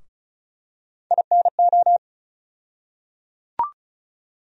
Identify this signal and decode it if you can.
Morse code, heard as a keyed tone, about 36 wpm, 700 Hz. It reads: INO
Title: USA Call Sign Postfixes - 3 Letters 35wpm